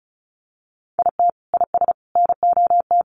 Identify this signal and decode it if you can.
Morse code, heard as a keyed tone, about 34 wpm, 700 Hz. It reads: IT IS NOT
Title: Sets of 3 Words 35wpm